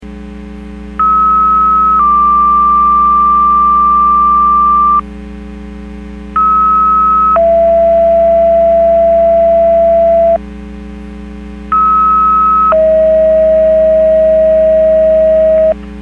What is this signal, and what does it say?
Two-tone page, heard as a one-second tone, then a three-second tone, about 1290 then 1180 Hz, about 1290 then 670 Hz, about 1290 then 640 Hz. It is heard from 1.0 to 15.7 s.
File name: Portage 154.385 Paging